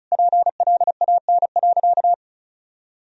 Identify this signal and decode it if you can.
Morse code, heard as a keyed tone, about 36 wpm, 700 Hz. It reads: PLAN.
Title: Sentences from Top 1000 Words - Review Part 2 - 35wpm